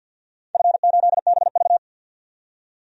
Morse code, heard as a keyed tone, about 49 wpm, 700 Hz. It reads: F8BV
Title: DXCC-100 Call Signs - 4 Characters - 50wpm